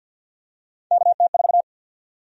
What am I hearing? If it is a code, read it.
Morse code, heard as a keyed tone, about 52 wpm, 700 Hz. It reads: KT4
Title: USA Call Sign Prefixes - Repeated 2x - 50wpm